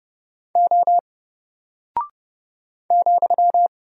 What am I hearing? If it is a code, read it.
Morse code, heard as a keyed tone, about 30 wpm, 700 Hz. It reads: O ,
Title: Single Character (Letter:Number:Punctuation:Prosign) 30wpm